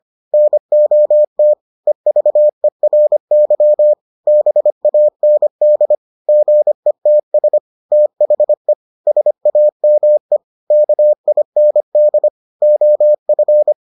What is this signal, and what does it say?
Morse code, heard as a keyed tone, about 25 wpm, 600 Hz. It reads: NOT EVERY BAND GETS THE SAME KIND OF